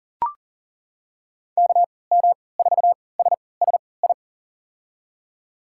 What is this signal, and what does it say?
Morse code, heard as a keyed tone, about 38 wpm, spaced as for 22 wpm, 700 Hz. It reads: KM4SSI
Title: USA Call Signs - 6 Characters - 3x Character Spacing - 40wpm